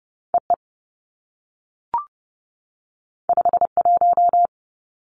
Morse code, heard as a keyed tone, about 30 wpm, 700 Hz. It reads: E E 5 1
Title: Two Challenging Letter-Number Combinations 30wpm